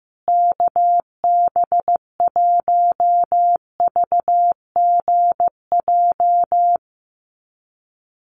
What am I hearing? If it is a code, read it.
Morse code, heard as a keyed tone, about 15 wpm, 700 Hz. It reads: KB1VGJ
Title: USA Call Signs Repeated 2x 15wpm